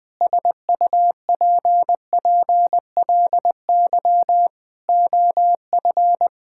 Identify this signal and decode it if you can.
Morse code, heard as a keyed tone, about 20 wpm, 700 Hz. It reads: SUPPLY OF
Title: Sentences from Top 1000 Words - Review Part 5 - 20wpm